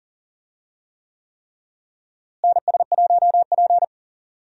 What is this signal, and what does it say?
Morse code, heard as a keyed tone, about 40 wpm, 700 Hz. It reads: NS1P
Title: USA Call Signs - 4 Characters - Repeated 3x - 40wpm